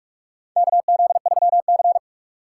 Morse code, heard as a keyed tone, about 42 wpm, 700 Hz. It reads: KZ3C
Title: USA Call Signs - 4 Characters - 3x Character Spacing - 45wpm